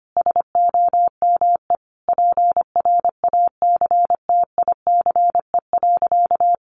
Morse code, heard as a keyed tone, about 25 wpm, 700 Hz. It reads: SOME PRACTICE.
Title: Sentences from Top 900 Words 25wpm